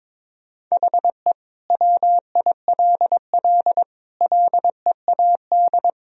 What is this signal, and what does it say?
Morse code, heard as a keyed tone, about 22 wpm, 700 Hz. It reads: HE WILL LEAD
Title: Sentences from Top 1000 Words - Review Part 2 - 22wpm